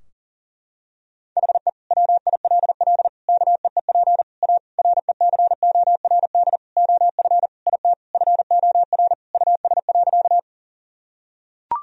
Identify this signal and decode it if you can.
Morse code, heard as a keyed tone, about 37 wpm, 700 Hz. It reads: HE WILL KEEP A RECORD OF IT FOR US.
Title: Sentences from Top 1000 Words - Review Part 3 - 40wpm